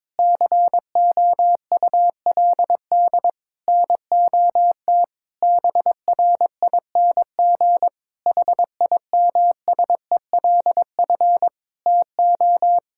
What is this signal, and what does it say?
Morse code, heard as a keyed tone, about 22 wpm, 700 Hz. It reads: COULD NOT BRING HIMSELF TO
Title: Sentences from Top 1600 Words 22wpm